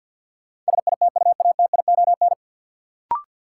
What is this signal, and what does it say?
Morse code, heard as a keyed tone, about 53 wpm, 700 Hz. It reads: SITUATION
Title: Sentences from Top 1200 Words - New Word Drill - 50wpm